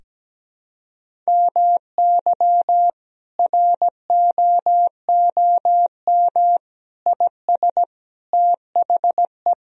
Morse code, heard as a keyed tone, about 17 wpm, 700 Hz. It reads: MY ROOM IS THE
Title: Sentences from Top 1000 Words - Review Part 1 - 17wpm